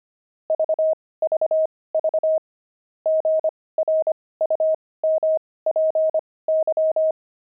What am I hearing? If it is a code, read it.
Morse code, heard as a keyed tone, about 25 wpm, spaced as for 18 wpm, 630 Hz. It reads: VVV GRUMPY
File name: pangram-fs-18-cs-25-grumpy